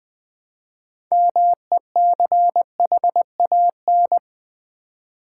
Morse code, heard as a keyed tone, about 20 wpm, 700 Hz. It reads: MECHAN
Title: Build A Word - 3 Letters At A Time - 9-Letter Words - 20wpm